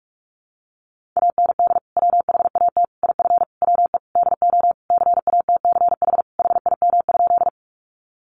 Morse code, heard as a keyed tone, about 45 wpm, 700 Hz. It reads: AND WHAT IF WE DO CATCH HIM?
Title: Sentences from Top 1500 Words 45wpm